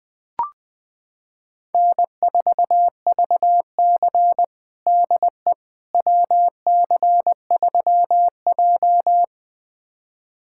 Morse code, heard as a keyed tone, about 20 wpm, 700 Hz. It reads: N4VC DE WC3J
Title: USA Callsign DE Callsign - 4 Characters - 20wpm